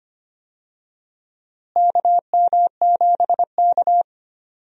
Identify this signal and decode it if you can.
Morse code, heard as a keyed tone, about 25 wpm, 700 Hz. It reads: KM7K
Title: USA Call Signs - 4 Characters - Repeated 3x - 25wpm